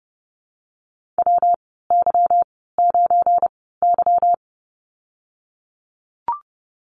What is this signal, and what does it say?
Morse code, heard as a keyed tone, about 30 wpm, spaced as for 17 wpm, 700 Hz. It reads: WY9Y
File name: Lesson 37g - Letter x - Call Signs 30wpm